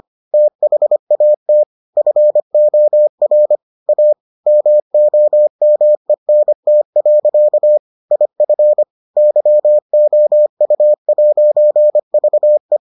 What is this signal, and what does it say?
Morse code, heard as a keyed tone, about 25 wpm, 600 Hz. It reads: THAT FOR A MOMENT. IF YOU'VE